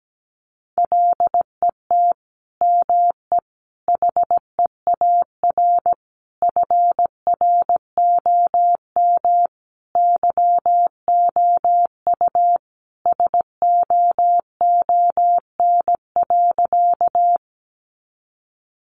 Morse code, heard as a keyed tone, about 17 wpm, 700 Hz. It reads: LET ME HEAR FROM YOU SOON.
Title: Sentences from Top 400 Words 17wpm